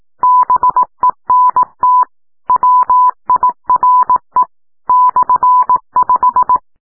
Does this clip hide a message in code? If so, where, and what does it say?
Yes: Morse code from 0.2 to 6.6 s, heard as a keyed tone, about 18 wpm, 1000 Hz. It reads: BENT WIRE /5